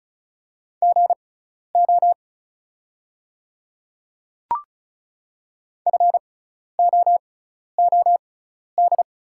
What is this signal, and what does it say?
Morse code, heard as a keyed tone, about 34 wpm, spaced as for 12 wpm, 700 Hz. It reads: GO FOOD
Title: 6x Character Spacing - Top 500 Words - 35wpm